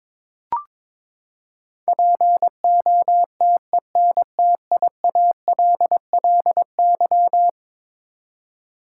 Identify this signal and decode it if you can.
Morse code, heard as a keyed tone, about 22 wpm, 700 Hz. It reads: POTENTIALLY
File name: practice / 22wpm / Eleven Letter Words - 22wpm